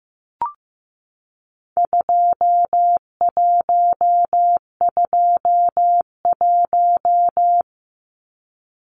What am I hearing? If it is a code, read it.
Morse code, heard as a keyed tone, about 15 wpm, 700 Hz. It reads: 2121
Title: Numbers - 4-Digit 15wpm